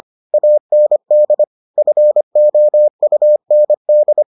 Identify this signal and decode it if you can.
Morse code, heard as a keyed tone, about 25 wpm, 600 Hz. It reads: AND FOUND